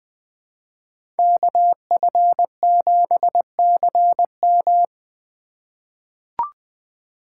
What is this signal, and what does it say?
Morse code, heard as a keyed tone, about 20 wpm, 700 Hz. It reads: KF7CM
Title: USA Call Signs - 5 Characters - 4x Character Spacing - 20wpm